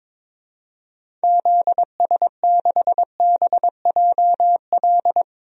Morse code, heard as a keyed tone, about 22 wpm, 700 Hz. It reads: ZS6BJL
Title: DXCC-100 Call Signs - 6 Characters - 22wpm